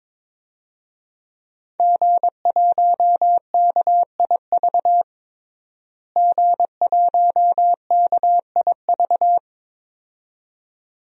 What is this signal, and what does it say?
Morse code, heard as a keyed tone, about 22 wpm, 700 Hz. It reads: G1KIV G1KIV
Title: DXCC-100 Call Signs - 5 Characters - Repeated 2x - 22wpm